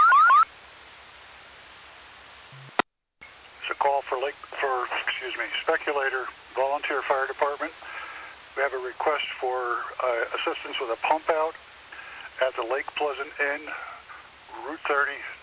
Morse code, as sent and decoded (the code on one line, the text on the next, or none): none